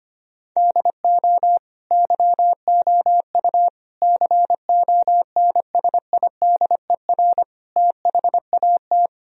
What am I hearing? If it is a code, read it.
Morse code, heard as a keyed tone, about 25 wpm, 700 Hz. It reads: DO YOU CONSIDER THAT
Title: Sentences from Top 1000 Words - Review Part 4 - 25wpm